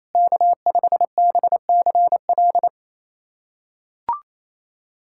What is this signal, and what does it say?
Morse code, heard as a keyed tone, about 28 wpm, 700 Hz. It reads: K5BCL
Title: USA Callsign DE Callsign - 5 Characters - 28wpm